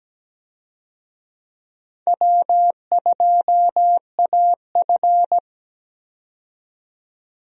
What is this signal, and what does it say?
Morse code, heard as a keyed tone, about 17 wpm, 700 Hz. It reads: W2AF
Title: USA Call Signs - 4 Characters - 17wpm